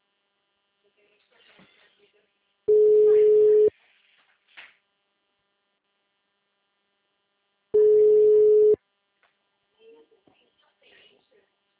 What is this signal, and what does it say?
Ringback tone (Europe, 425 Hz), with speech.